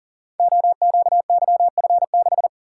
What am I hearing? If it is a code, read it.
Morse code, heard as a keyed tone, about 39 wpm, 700 Hz. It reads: OQYF6
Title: Code Groups - 5 Letter:Numbers - 40wpm